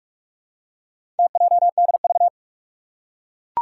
Morse code, heard as a keyed tone, about 44 wpm, 700 Hz. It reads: TJDV
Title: Four Challenging Letter-Number Combinations 45wpm